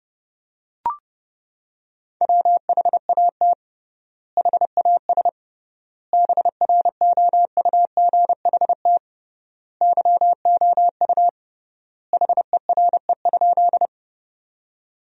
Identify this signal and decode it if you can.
Morse code, heard as a keyed tone, about 32 wpm, 700 Hz. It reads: WHAT HAS BROUGHT YOU HERE?
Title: 3x Word-Spacing - Sentences from Top 500 Words - 30wpm